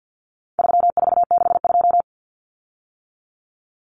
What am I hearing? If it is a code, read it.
Morse code, heard as a keyed tone, about 50 wpm, 700 Hz. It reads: 3462
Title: Common Cut Numbers - 4-Digit 50wpm